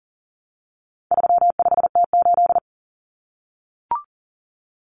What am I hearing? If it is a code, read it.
Morse code, heard as a keyed tone, about 40 wpm, 700 Hz. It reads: 35T8
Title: Common Cut Numbers - 4-Digit 40wpm